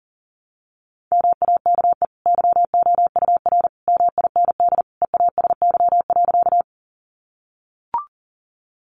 Morse code, heard as a keyed tone, about 40 wpm, 700 Hz. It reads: MAKE YOUR MIND EASY.
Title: Sentences from Top 700 Words 40wpm